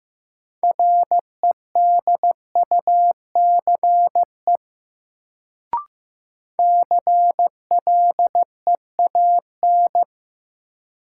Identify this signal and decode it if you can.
Morse code, heard as a keyed tone, about 15 wpm, 700 Hz. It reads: REDUCE CLEAN
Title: Sentences from Top 1100 Words - New Word Drill - 15wpm